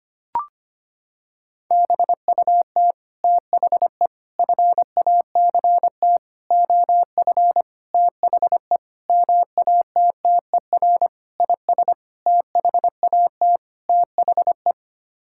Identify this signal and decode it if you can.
Morse code, heard as a keyed tone, about 25 wpm, 700 Hz. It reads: BUT THE FACT OF THE MATTER IS THAT THE
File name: Sets of 9 Words - 25wpm